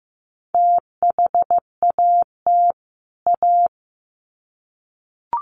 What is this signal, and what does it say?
Morse code, heard as a keyed tone, about 15 wpm, 700 Hz. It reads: THAT A
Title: Sets of 2 Words 15wpm